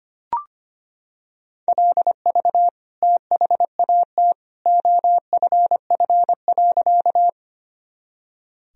Morse code, heard as a keyed tone, about 25 wpm, 700 Hz. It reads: LV THAT OFF.
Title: Sentences from Top 400 Words and Abbreviations 25wpm